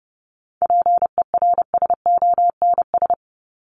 Morse code, heard as a keyed tone, about 30 wpm, 700 Hz. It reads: PERSONS
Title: ES:S Suffix - Top 100 Words with Base Words 30wpm